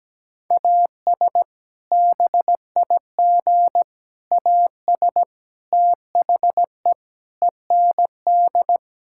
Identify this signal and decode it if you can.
Morse code, heard as a keyed tone, about 17 wpm, 700 Hz. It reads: AS BIG AS THE END
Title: Sentences from Top 1600 Words 17wpm